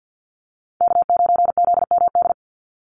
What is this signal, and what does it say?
Morse code, heard as a keyed tone, about 50 wpm, 700 Hz. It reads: K9ZMD
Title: NA SOTA Chaser Callsigns 50wpm